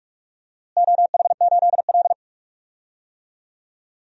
Morse code, heard as a keyed tone, about 41 wpm, 700 Hz. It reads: OH8L
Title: DXCC-100 Call Signs - 4 Characters - Repeated 2x - 45wpm